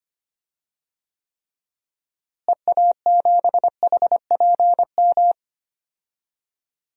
Morse code, heard as a keyed tone, about 25 wpm, 700 Hz. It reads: EA7HPM
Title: DXCC-100 Call Signs - 6 Characters - Repeated 3x - 25wpm